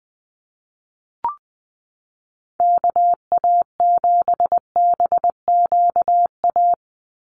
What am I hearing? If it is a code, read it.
Morse code, heard as a keyed tone, about 20 wpm, 700 Hz. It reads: KA7BQA
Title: USA Call Signs 20wpm